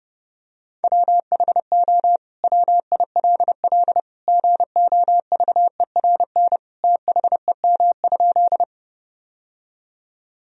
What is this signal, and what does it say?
Morse code, heard as a keyed tone, about 30 wpm, 700 Hz. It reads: WHO WILL GOVERN THEM?
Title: Sentences from Top 1000 Words - Review Part 2 - 30wpm